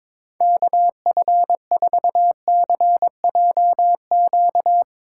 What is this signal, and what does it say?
Morse code, heard as a keyed tone, about 22 wpm, 700 Hz. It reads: KF4CJQ
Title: USA Callsign DE Callsign - 6 Characters - 22wpm